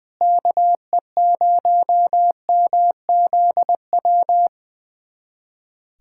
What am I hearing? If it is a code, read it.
Morse code, heard as a keyed tone, about 21 wpm, 700 Hz. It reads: KE0MZW
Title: USA Callsign DE Callsign 20wpm